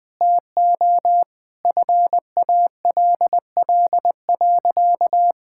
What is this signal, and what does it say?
Morse code, heard as a keyed tone, about 19 wpm, 700 Hz. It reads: TO FALL.
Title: Sentences from Top 1000 Words - Review Part 6 - 20wpm